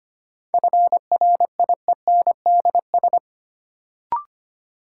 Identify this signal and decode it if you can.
Morse code, heard as a keyed tone, about 25 wpm, 700 Hz. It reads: FRIENDS
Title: Sentences from Top 1000 Words - Review Part 5 - New Word Drill - 25wpm